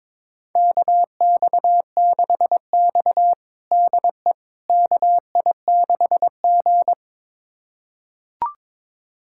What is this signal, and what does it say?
Morse code, heard as a keyed tone, about 23 wpm, 700 Hz. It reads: KX6X DE KI6G
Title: USA Callsign DE Callsign - 4 Characters - 22wpm